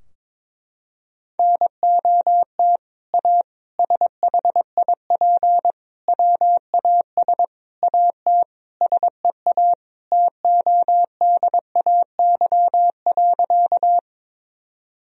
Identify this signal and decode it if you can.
Morse code, heard as a keyed tone, about 22 wpm, 700 Hz. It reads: NOT A SHIP WAS AT SEA TODAY.
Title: Sentences from Top 1000 Words - Review Part 2 - 22wpm